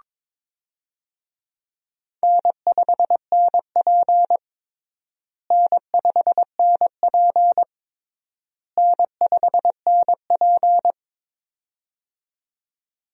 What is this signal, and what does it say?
Morse code, heard as a keyed tone, about 21 wpm, 700 Hz. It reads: N5NP N5NP N5NP
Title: USA Call Signs - 4 Characters - Repeated 3x - 22wpm